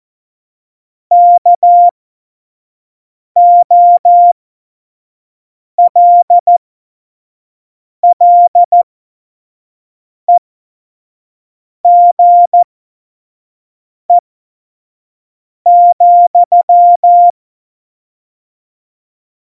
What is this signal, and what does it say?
Morse code, heard as a keyed tone, about 14 wpm, spaced as for 5 wpm, 700 Hz. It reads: KOLLEGE,
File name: Morsekurs / Witze / Witz 03. November 2020, 25 F70